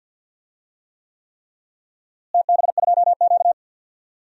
Morse code, heard as a keyed tone, about 52 wpm, 700 Hz. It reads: TB2Q
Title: DXCC-100 Call Signs - 4 Characters - Repeated 3x - 50wpm